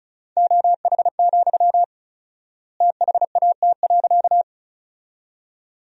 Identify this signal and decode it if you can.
Morse code, heard as a keyed tone, about 34 wpm, 700 Hz. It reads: OH, THAT.
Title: 4x Word-Spacing - Sentences from Top 500 Words - 35wpm